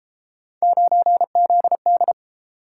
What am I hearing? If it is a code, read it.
Morse code, heard as a keyed tone, about 28 wpm, 700 Hz. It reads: 9ZD